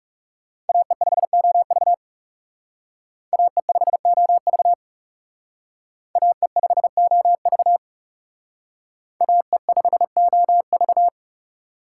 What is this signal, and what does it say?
Morse code, heard as a keyed tone, about 39 wpm, 700 Hz. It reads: AE5OV AE5OV AE5OV AE5OV